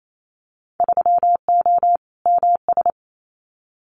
Morse code, heard as a keyed tone, about 28 wpm, 700 Hz. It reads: 3O MS
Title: Winter Field Day Exchanges Encore 28wpm